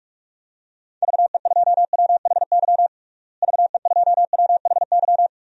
Morse code, heard as a keyed tone, about 41 wpm, 700 Hz. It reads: VE2WHY VE2WHY
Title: DXCC-100 Call Signs - 6 Characters - Repeated 3x - 45wpm